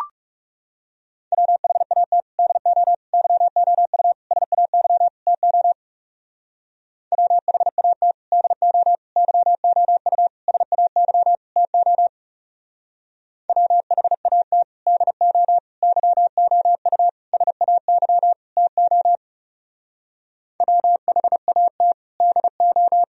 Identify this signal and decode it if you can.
Morse code, heard as a keyed tone, about 36 wpm, 700 Hz. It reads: WHAT DO YOU SAY TO WHAT DO YOU SAY TO WHAT DO YOU SAY TO WHAT DO
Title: Speed-Racing - Sets of 5 Words 30wpm